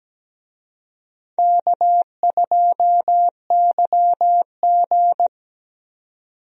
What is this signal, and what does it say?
Morse code, heard as a keyed tone, about 17 wpm, 700 Hz. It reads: K2YG
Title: USA Call Signs - 4 Characters - 17wpm